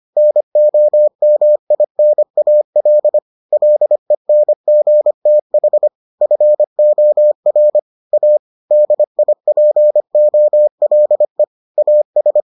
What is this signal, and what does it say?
Morse code, heard as a keyed tone, about 25 wpm, 600 Hz. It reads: NOMINAL LENGTH FOR A DIPOLE AS